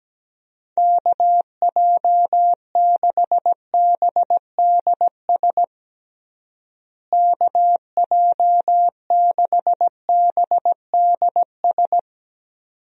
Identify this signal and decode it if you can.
Morse code, heard as a keyed tone, about 17 wpm, 700 Hz. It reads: KJ6BDS KJ6BDS
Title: USA Call Signs - 6 Characters - Repeated 2x - 17wpm